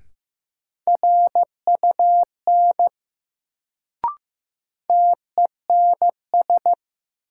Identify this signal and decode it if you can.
Morse code, heard as a keyed tone, about 15 wpm, 700 Hz. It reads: RUN TENS
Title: Sets of 4 Words - New Word Drill - 15wpm